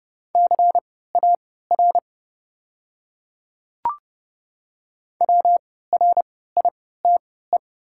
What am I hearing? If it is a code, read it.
Morse code, heard as a keyed tone, about 29 wpm, spaced as for 17 wpm, 700 Hz. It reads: CAR WRITE
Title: 3x Character Spacing - Top 500 Words - 30wpm